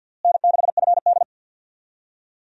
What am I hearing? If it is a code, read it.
Morse code, heard as a keyed tone, about 51 wpm, 700 Hz. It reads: N6FD
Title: USA Call Signs - 4 Characters - Repeated 2x - 50wpm